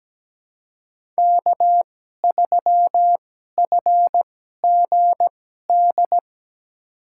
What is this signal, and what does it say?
Morse code, heard as a keyed tone, about 17 wpm, spaced as for 12 wpm, 700 Hz. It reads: K3FGD
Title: USA Call Signs - 5 Characters - 2x Character Spacing - 17wpm